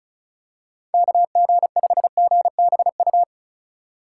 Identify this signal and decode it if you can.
Morse code, heard as a keyed tone, about 36 wpm, 700 Hz. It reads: KG5GBU
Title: USA Call Signs - 3x Character Spacing - 35wpm